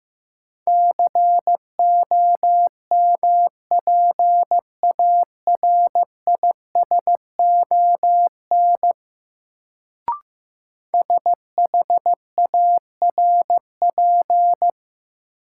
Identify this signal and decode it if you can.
Morse code, heard as a keyed tone, about 15 wpm, 700 Hz. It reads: COMPARISON SHARP
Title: Sentences from Top 1600 Words - New Word Drill - 15wpm